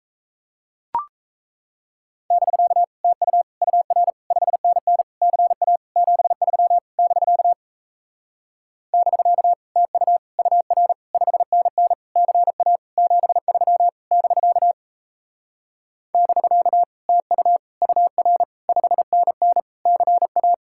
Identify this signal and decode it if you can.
Morse code, heard as a keyed tone, about 37 wpm, 700 Hz. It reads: <BK> TU UR 5NN CA 73 <BK> <BK> TU UR 5NN CA 73 <BK> <BK> TU UR 5NN CA